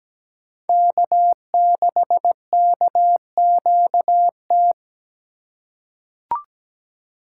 Morse code, heard as a keyed tone, about 17 wpm, 700 Hz. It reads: K6KQT
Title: USA Call Signs - 8x Character Spacing - 17wpm